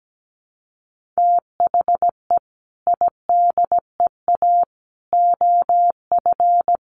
Morse code, heard as a keyed tone, about 17 wpm, 700 Hz. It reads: THE IDEA OF
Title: Sentences from Top 1100 Words 17wpm